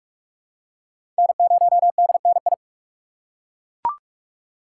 Morse code, heard as a keyed tone, about 40 wpm, 700 Hz. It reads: N0DNI